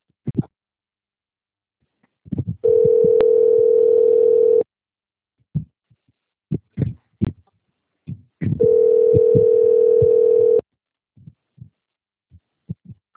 Ringback tone (North America), 440 + 480 Hz.